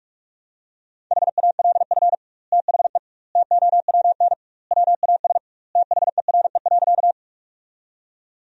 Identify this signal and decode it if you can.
Morse code, heard as a keyed tone, about 46 wpm, 700 Hz. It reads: HALF THE TOWN WAS THERE.
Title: 2x Word-Spacing - Sentences from Top 500 Words - 45wpm